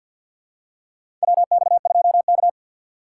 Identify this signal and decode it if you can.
Morse code, heard as a keyed tone, about 50 wpm, 700 Hz. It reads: WX2K
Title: USA Call Signs - 3x Character Spacing - 50wpm